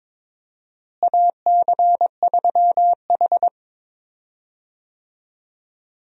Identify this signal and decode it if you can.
Morse code, heard as a keyed tone, about 22 wpm, 700 Hz. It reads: AC3H
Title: USA Call Signs - 4 Characters - Repeated 2x - 22wpm